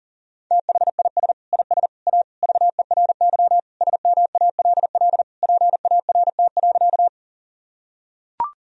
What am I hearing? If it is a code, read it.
Morse code, heard as a keyed tone, about 40 wpm, 700 Hz. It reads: THIS IS A VERY SMALL PART.
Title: Sentences from Top 1000 Words - Review Part 1 - 40wpm